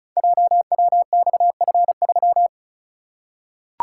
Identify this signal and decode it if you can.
Morse code, heard as a keyed tone, about 36 wpm, 690 Hz. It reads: JWXF3
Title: Code Groups - 5 Letter:Numbers - 35wpm